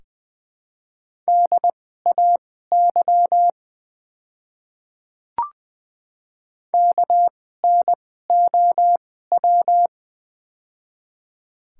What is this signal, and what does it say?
Morse code, heard as a keyed tone, about 20 wpm, spaced as for 14 wpm, 700 Hz. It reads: DAY KNOW
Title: 2x Character Spacing - Top 100 Words - 20wpm